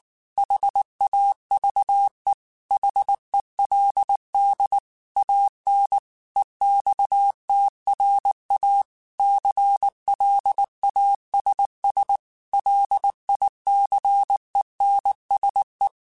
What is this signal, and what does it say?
Morse code, heard as a keyed tone, about 19 wpm, 800 Hz. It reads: HAVE HELD AN EXTRA CLASS LICENSE
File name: QSOs / QSOs-mp3s / WA7QOX 2@18